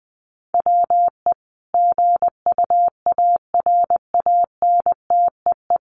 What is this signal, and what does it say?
Morse code, heard as a keyed tone, about 20 wpm, 700 Hz. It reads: WE GUARANTEE
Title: A Sound Of Thunder 20wpm